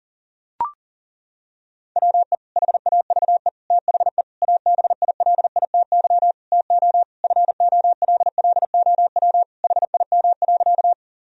Morse code, heard as a keyed tone, about 37 wpm, 700 Hz. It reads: WE HAVE THE ABILITY TO FOLLOW HIM.
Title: Sentences from Top 1000 Words - Review Part 3 - 40wpm